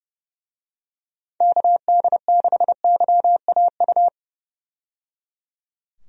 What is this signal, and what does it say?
Morse code, heard as a keyed tone, about 30 wpm, 700 Hz. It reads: KD6YAU
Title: USA Call Signs - 6 Characters - 30wpm